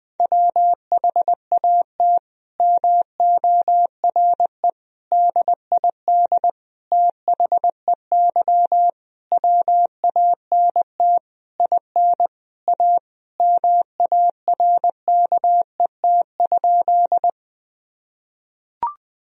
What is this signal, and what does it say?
Morse code, heard as a keyed tone, about 20 wpm, 700 Hz. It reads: WHAT MORE DID THEY WANT IN A MARKET?
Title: Sentences from Top 1000 Words - Review Part 3 - 20wpm